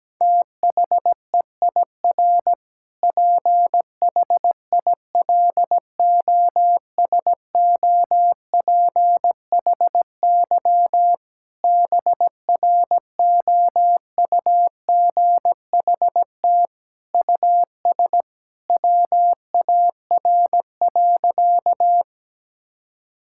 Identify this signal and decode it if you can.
Morse code, heard as a keyed tone, about 17 wpm, 700 Hz. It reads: THEIR PHILOSOPHY BROUGHT US WAR.
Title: Sentences from Top 1000 Words - Review Part 6 - 17wpm